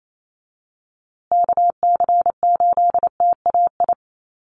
Morse code, heard as a keyed tone, about 28 wpm, 700 Hz. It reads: KC8TAI
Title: USA Call Signs Repeated 3x 28wpm